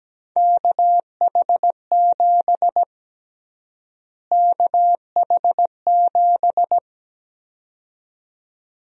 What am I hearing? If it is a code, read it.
Morse code, heard as a keyed tone, about 17 wpm, 700 Hz. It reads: KH7 KH7
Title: USA Call Sign Prefixes - Repeated 2x - 17wpm